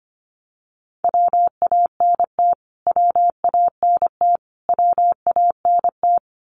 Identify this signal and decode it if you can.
Morse code, heard as a keyed tone, about 25 wpm, 700 Hz. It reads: WANT WANT WANT
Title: Top 300 Words Repeated 25wpm